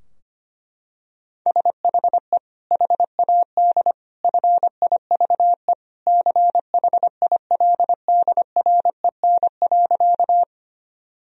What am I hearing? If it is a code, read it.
Morse code, heard as a keyed tone, about 25 wpm, 700 Hz. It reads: SHE HAD FIVE CHILDREN.
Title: Sentences from Top 1000 Words - Review Part 2 - 25wpm